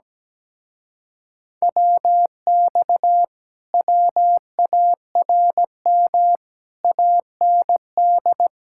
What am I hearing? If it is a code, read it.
Morse code, heard as a keyed tone, about 17 wpm, 700 Hz. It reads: WX WARM AND
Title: Speed-Racing Common QSO Phrases - 17wpm